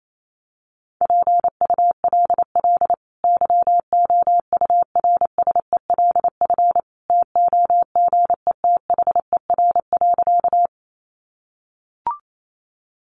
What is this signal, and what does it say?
Morse code, heard as a keyed tone, about 28 wpm, 700 Hz. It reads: PULL YOURSELF TOGETHER.
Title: Sentences from Top 600 Words 28wpm